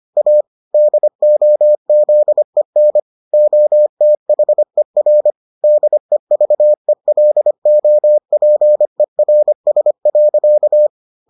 Morse code, heard as a keyed tone, about 25 wpm, 600 Hz. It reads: A DOZEN OTHER DEVELOPERS.